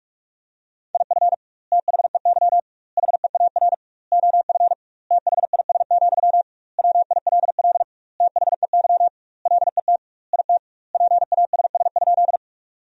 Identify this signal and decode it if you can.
Morse code, heard as a keyed tone, about 44 wpm, 700 Hz. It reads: IF THEY HEAR OF THIS, WILL THEY LET IT PASS?
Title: 2x Word-Spacing - Sentences from Top 400 Words - 45wpm